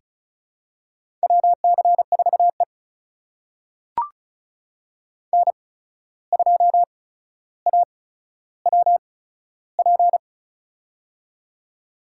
Morse code, heard as a keyed tone, about 36 wpm, spaced as for 14 wpm, 700 Hz. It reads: WC4E N2AWP
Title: USA Call Signs - 8x Character Spacing - 35wpm